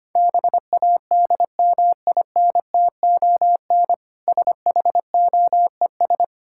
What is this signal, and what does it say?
Morse code, heard as a keyed tone, about 25 wpm, 700 Hz. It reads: BADMINTON SHOES
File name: practice / 25wpm / Difficult Words - Speed-Racing with Hints - 25wpm